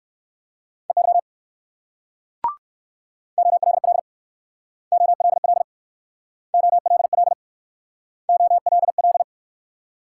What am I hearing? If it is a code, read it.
Morse code, heard as a keyed tone, about 64 wpm, 690 Hz. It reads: EY OLL OLL OLL OLL